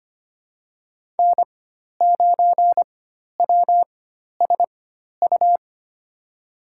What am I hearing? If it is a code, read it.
Morse code, heard as a keyed tone, about 25 wpm, spaced as for 12 wpm, 700 Hz. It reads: N9WSU
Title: USA Call Signs - 5 Characters - 4x Character Spacing - 25wpm